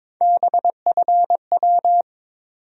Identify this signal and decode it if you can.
Morse code, heard as a keyed tone, about 22 wpm, 700 Hz. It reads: BFW
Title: USA Call Sign Postfixes - 3 Letters 22wpm